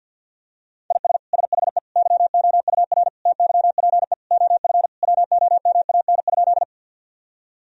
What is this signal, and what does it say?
Morse code, heard as a keyed tone, about 52 wpm, 700 Hz. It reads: IS SHE YOUR TYPE OF WOMAN?
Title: Sentences from Top 1000 Words - Review Part 3 - 50wpm